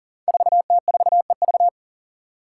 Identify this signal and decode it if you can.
Morse code, heard as a keyed tone, about 39 wpm, 690 Hz. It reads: 4T4EV
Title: Five Challenging Letter-Number Combinations 40wpm